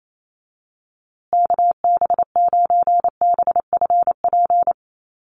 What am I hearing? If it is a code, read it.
Morse code, heard as a keyed tone, about 28 wpm, 700 Hz. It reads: KB9BFP
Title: USA Call Signs Repeated 3x 28wpm